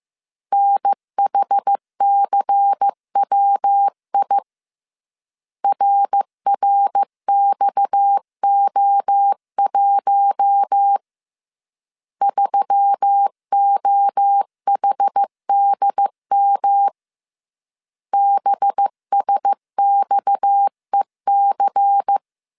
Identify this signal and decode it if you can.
Morse code, heard as a keyed tone, about 15 wpm, 800 Hz. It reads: NHCWI RRXO1 3OHDM BSXEC